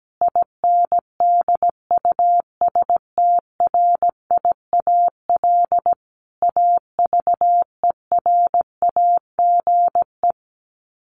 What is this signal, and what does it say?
Morse code, heard as a keyed tone, about 17 wpm, 700 Hz. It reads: INDUSTRIAL AVERAGE
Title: Sets of 5 Words 17wpm